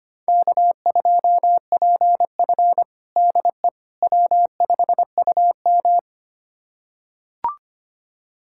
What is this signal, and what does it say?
Morse code, heard as a keyed tone, about 25 wpm, 700 Hz. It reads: K2PF DE W5UM
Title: USA Callsign DE Callsign - 4 Characters - 25wpm